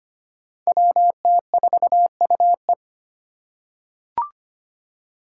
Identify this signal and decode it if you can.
Morse code, heard as a keyed tone, about 25 wpm, 700 Hz. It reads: WT4UE
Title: Five Challenging Letter-Number Combinations 25wpm